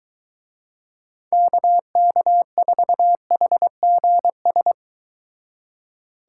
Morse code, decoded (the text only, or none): KK4HGS